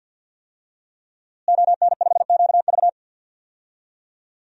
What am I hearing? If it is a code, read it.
Morse code, heard as a keyed tone, about 51 wpm, 700 Hz. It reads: ON5QV
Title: DXCC-100 Call Signs - 5 Characters - Repeated 2x - 50wpm